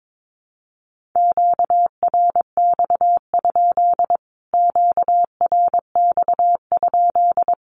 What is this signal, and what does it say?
Morse code, heard as a keyed tone, about 22 wpm, 700 Hz. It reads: QRX? QRX?
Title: Q-Codes Repeated 3x 22wpm